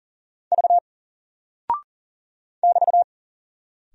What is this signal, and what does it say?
Morse code, heard as a keyed tone, about 40 wpm, 700 Hz. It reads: V <BT>
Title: Single Character (Letter:Number:Punctuation:Prosign) Alt 40wpm